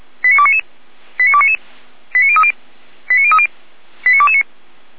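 Five-tone selective call (ZVEI), heard as quick runs of five tones, tones of about 80 ms. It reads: 89299 89299 89030 89030 89229
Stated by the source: Received via SDR as a ZVEI five-tone call, address 89165, on the Feuerwehr Orlamünde fire brigade audio server.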